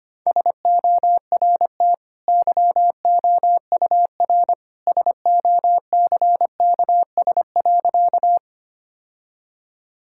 Morse code, heard as a keyed tone, about 25 wpm, 700 Hz. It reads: SORT YOUR SOCKS.
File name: Easy Sentences with Hints - 25wpm